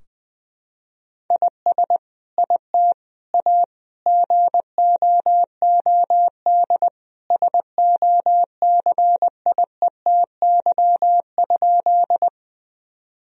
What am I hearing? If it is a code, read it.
Morse code, heard as a keyed tone, about 20 wpm, 700 Hz. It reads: IS IT A GOOD SOCIETY?
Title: Sentences from Top 1000 Words - Review Part 4 - 20wpm